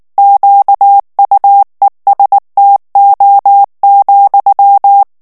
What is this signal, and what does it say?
Morse code, heard as a keyed tone, about 19 wpm, 800 Hz. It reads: QUESTO,
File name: morse mp3 / ITALIEN texte clair 2 vitesse 18 mots minute